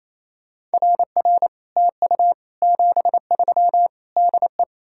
Morse code, heard as a keyed tone, about 38 wpm, spaced as for 33 wpm, 700 Hz. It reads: RR TU 73 DE
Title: Speed-Racing Common QSO Phrases Encore - 28wpm